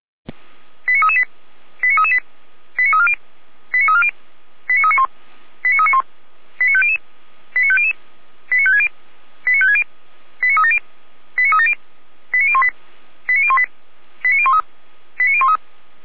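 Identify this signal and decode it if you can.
Five-tone selective call (ZVEI), heard as quick runs of five tones, tones of about 70 ms. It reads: 89338 89338 89340 89340 89391 89391 89500 89500 89560 89560 89370 89370 89018 89018 89013 89013
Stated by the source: ZVEI five-tone selective call 89327 received by SDR.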